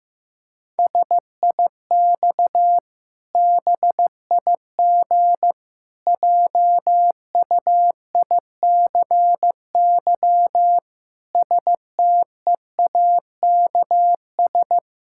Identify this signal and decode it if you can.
Morse code, heard as a keyed tone, about 15 wpm, 700 Hz. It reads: SIX BIG JUICY STEAKS
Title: Pangram Challenge v2 - 15wpm